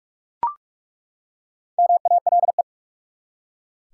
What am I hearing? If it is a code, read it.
Morse code, heard as a keyed tone, about 43 wpm, 700 Hz. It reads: MALE
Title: Four Letter Words - 45wpm